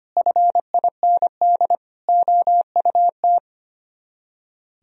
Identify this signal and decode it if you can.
Morse code, heard as a keyed tone, about 33 wpm, 700 Hz. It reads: FIND OUT